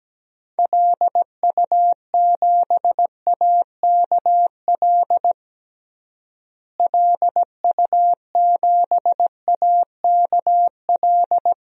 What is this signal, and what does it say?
Morse code, heard as a keyed tone, about 17 wpm, 700 Hz. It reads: LU7AKL LU7AKL
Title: DXCC-100 Call Signs - 6 Characters - Repeated 3x - 17wpm